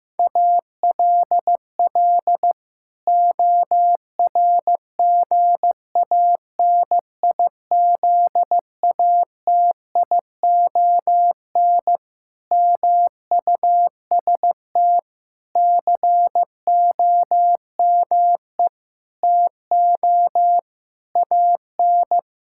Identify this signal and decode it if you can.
Morse code, heard as a keyed tone, about 15 wpm, 700 Hz. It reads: ALL ORGANIZATION MUST COME TO AN
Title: Sentences from Top 1000 Words - Review Part 5 - 15wpm